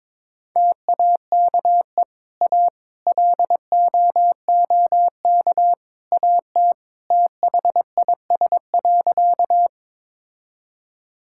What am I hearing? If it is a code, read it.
Morse code, heard as a keyed tone, about 22 wpm, 700 Hz. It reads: TAKE A LOOK AT THIS.
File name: practice / 22wpm / Sentences with 3-Word Phrases - 22wpm